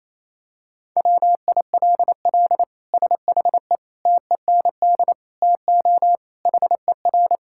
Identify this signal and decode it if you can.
Morse code, heard as a keyed tone, about 28 wpm, 700 Hz. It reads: WILL SHE TEND TO HER
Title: Sentences from Top 1000 Words - Review Part 5 - 28wpm